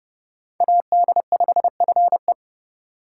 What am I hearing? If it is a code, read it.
Morse code, heard as a keyed tone, about 30 wpm, 700 Hz. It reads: AD5FE